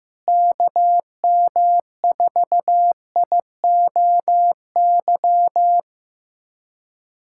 Morse code, heard as a keyed tone, about 16 wpm, 700 Hz. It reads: KM4IOY
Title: USA Callsign DE Callsign - 6 Characters - 15wpm